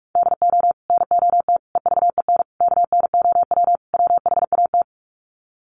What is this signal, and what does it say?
Morse code, heard as a keyed tone, about 44 wpm, 700 Hz. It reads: DO NOT EVEN KNOW WHAT
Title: Sets of 5 Words Encore 45wpm